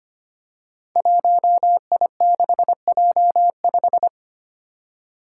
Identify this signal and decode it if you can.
Morse code, heard as a keyed tone, about 25 wpm, 700 Hz. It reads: 1I6J5
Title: Five Challenging Letter-Number Combinations 25wpm